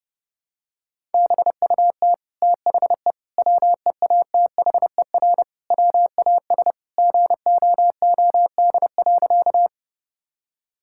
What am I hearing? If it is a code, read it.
Morse code, heard as a keyed tone, about 30 wpm, 700 Hz. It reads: BUT THE WEATHER WAS GOOD.
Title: Sentences from Top 1000 Words - Review Part 5 - 30wpm